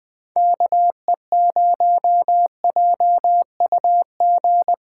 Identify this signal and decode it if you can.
Morse code, heard as a keyed tone, about 20 wpm, 700 Hz. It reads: KE0JUG
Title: USA Call Signs - 6 Characters - 3x Character Spacing - 20wpm